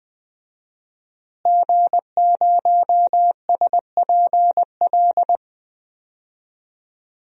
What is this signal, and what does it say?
Morse code, heard as a keyed tone, about 20 wpm, 700 Hz. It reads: G0SPL
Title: DXCC-100 Call Signs - 5 Characters - Repeated 3x - 20wpm